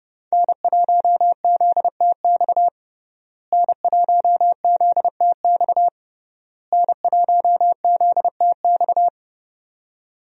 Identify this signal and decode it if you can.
Morse code, heard as a keyed tone, about 30 wpm, 700 Hz. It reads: N1ZTX N1ZTX N1ZTX
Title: USA Call Signs - 5 Characters - Repeated 3x - 30wpm